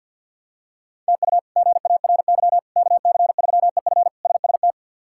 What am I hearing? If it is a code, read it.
Morse code, heard as a keyed tone, about 53 wpm, 700 Hz. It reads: TU GARY KC3EF SST
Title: K1USN SST QSO Practice - 50wpm